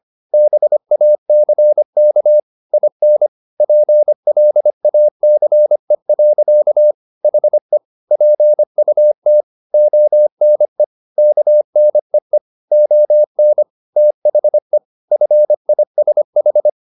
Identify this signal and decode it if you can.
Morse code, heard as a keyed tone, about 25 wpm, 600 Hz. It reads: BACK IN PLACE. HE PUT ONE KNEE ON THE FISH
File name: the old man and the sea part 1 25wpm